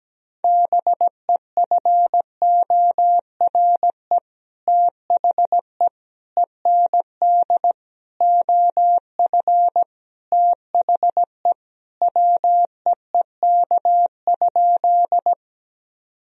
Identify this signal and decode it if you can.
Morse code, heard as a keyed tone, about 17 wpm, 700 Hz. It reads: BEFORE THE END OF THE WEEK?
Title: Sentences from Top 1000 Words - Review Part 2 - 17wpm